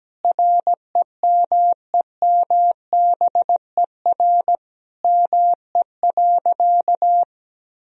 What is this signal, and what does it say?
Morse code, heard as a keyed tone, about 17 wpm, 700 Hz. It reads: REMEMBER ME.
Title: Sentences from Top 1000 Words - Review Part 1 - 17wpm